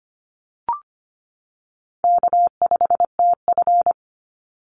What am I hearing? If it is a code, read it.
Morse code, heard as a keyed tone, about 25 wpm, 700 Hz. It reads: K5TF
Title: Easy Call Signs - 25wpm